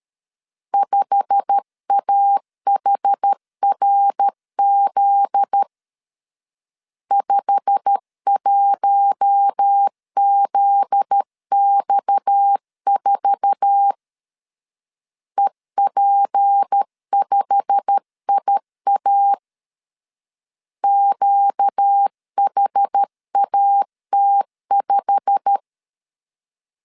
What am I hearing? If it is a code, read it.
Morse code, heard as a keyed tone, about 13 wpm, 800 Hz. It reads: 5AHRZ 51ZX4 EP5IA QHAT5
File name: LEZIONI / 8 YJ45 / 8 YJ45 12wpm 2151